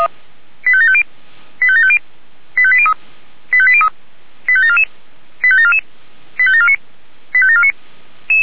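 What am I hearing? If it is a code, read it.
Five-tone selective call (ZVEI), heard as quick runs of five tones, tones of about 70 ms. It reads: 85750 85750 85803 85803 85744 85744 85749 85749
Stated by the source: ZVEI five-tone selective call received by SDR.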